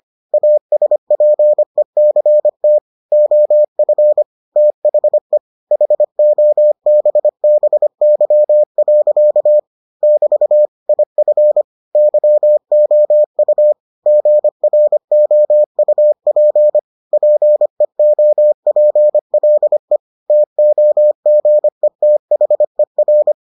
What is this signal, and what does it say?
Morse code, heard as a keyed tone, about 25 wpm, 600 Hz. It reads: ASPECT OF THE HOBBY. <BT> IF YOU GROUP PEOPLE TOGETHER